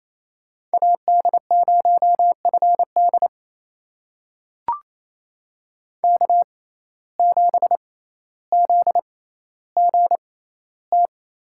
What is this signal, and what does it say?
Morse code, heard as a keyed tone, about 28 wpm, spaced as for 14 wpm, 700 Hz. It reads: AD0FD K7ZGT
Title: USA Call Signs - 5 Characters - 6x Character Spacing - 28wpm